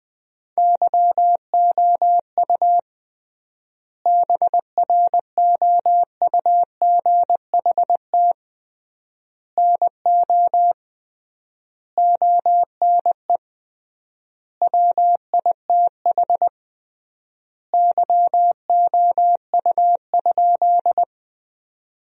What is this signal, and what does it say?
Morse code, heard as a keyed tone, about 20 wpm, 700 Hz. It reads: YOU BROUGHT NO ONE WITH YOU?
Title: 3x Word-Spacing - Sentences from Top 500 Words - 20wpm